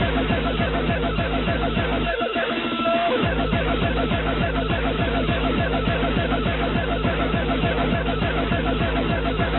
Two-tone teleprinter signal, about 200 Hz shift.